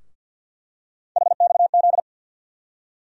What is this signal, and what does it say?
Morse code, heard as a keyed tone, about 50 wpm, 700 Hz. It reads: HXZ
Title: USA Call Sign Postfixes - 3 Letters 50wpm